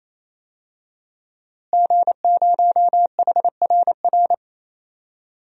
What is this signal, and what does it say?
Morse code, heard as a keyed tone, about 28 wpm, 700 Hz. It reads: G0HRR
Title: DXCC-100 Call Signs - 5 Characters - Repeated 3x - 28wpm